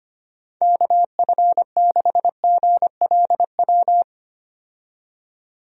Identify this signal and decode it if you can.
Morse code, heard as a keyed tone, about 25 wpm, 700 Hz. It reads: KF6GLW
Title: USA Call Signs - 6 Characters - 25wpm